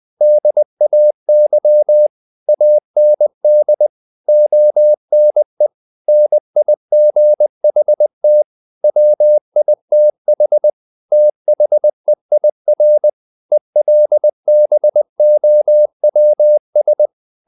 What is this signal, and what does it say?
Morse code, heard as a keyed tone, about 20 wpm, 600 Hz. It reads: DAY AND ONE NIGHT WITH THEIR ELBOWS
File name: the old man and the sea / the old man and the sea part 1 20wpm